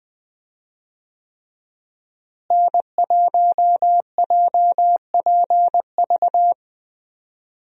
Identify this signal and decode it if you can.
Morse code, heard as a keyed tone, about 20 wpm, 700 Hz. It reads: N1JPV
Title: USA Call Signs - 5 Characters - Repeated 3x - 20wpm